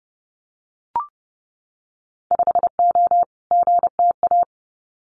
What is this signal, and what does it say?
Morse code, heard as a keyed tone, about 30 wpm, 700 Hz. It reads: 5O GTA
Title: Winter Field Day Exchanges Encore 30wpm